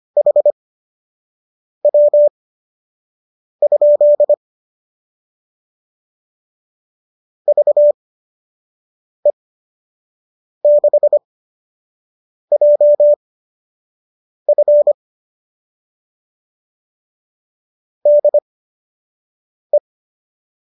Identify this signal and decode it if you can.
Morse code, heard as a keyed tone, about 25 wpm, spaced as for 6 wpm, 590 Hz. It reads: HW? VE6JF DE
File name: qso002 06